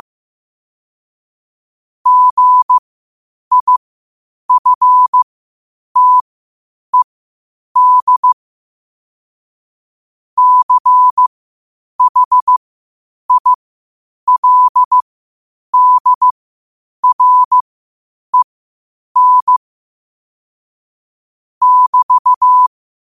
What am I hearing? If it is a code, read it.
Morse code, heard as a keyed tone, about 15 wpm, spaced as for 8 wpm, 1000 Hz. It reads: GIFTED CHILDREN <BT>
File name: qotdfiles / 8wpm